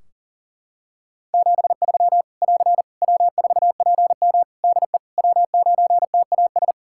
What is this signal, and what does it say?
Morse code, heard as a keyed tone, about 37 wpm, 700 Hz. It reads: 73 <AR> W4PM DE W9TAS
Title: K1USN SST QSO Practice 40wpm